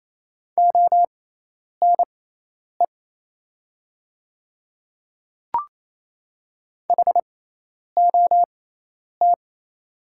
Morse code, heard as a keyed tone, about 27 wpm, spaced as for 10 wpm, 700 Hz. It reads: ONE HOT
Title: 6x Character Spacing - Top 100 Words - 28wpm